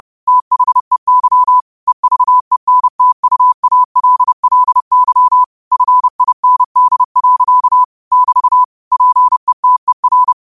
Morse code, heard as a keyed tone, about 30 wpm, 1000 Hz. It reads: THEY EVENTUALLY FIND. <BT> PETER